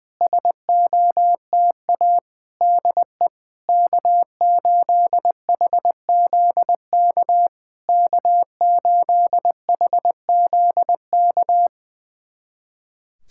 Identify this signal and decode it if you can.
Morse code, heard as a keyed tone, about 20 wpm, 700 Hz. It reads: SOTA DE K8HZK K8HZK
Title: SOTA QSO Practice 20wpm